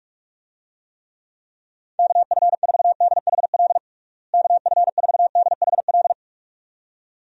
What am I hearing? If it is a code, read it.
Morse code, heard as a keyed tone, about 46 wpm, 700 Hz. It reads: KF4DHL KF4DHL
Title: USA Call Signs - 6 Characters - Repeated 2x - 45wpm